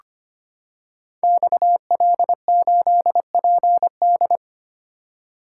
Morse code, heard as a keyed tone, about 25 wpm, 700 Hz. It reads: XL8PD
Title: Code Groups - 5 Letter:Numbers - 25wpm